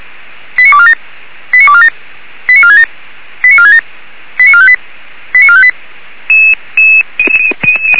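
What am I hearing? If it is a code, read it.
Five-tone selective call (ZVEI), heard as quick runs of five tones, tones of about 70 ms. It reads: 89247 89247 89467 89467 89458 89458
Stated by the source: ZVEI five-tone call received by SDR.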